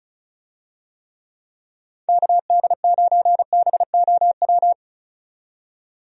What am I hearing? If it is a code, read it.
Morse code, heard as a keyed tone, about 36 wpm, 700 Hz. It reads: KD9BOW
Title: USA Call Signs - 6 Characters - 35wpm